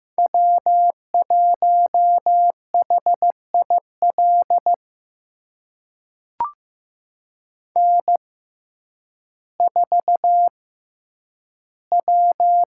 Morse code, heard as a keyed tone, about 15 wpm, spaced as for 9 wpm, 700 Hz. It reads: W1HIL N4W
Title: USA Call Signs - 5 Characters - 6x Character Spacing - 15wpm